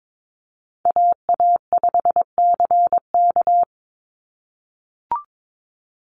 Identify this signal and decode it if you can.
Morse code, heard as a keyed tone, about 22 wpm, 700 Hz. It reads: AA5CK
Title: NA SOTA Chaser Callsigns 22wpm